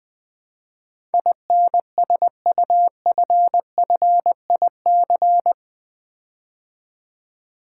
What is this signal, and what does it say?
Morse code, heard as a keyed tone, about 20 wpm, 700 Hz. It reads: INSUFFIC
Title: Build A Word - 4 Letters At A Time - 12-Letter Words - 20wpm